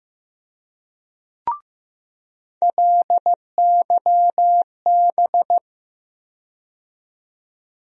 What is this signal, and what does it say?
Morse code, heard as a keyed tone, about 15 wpm, 700 Hz. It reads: LYB
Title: USA Call Sign Postfixes - 3 Letters 15wpm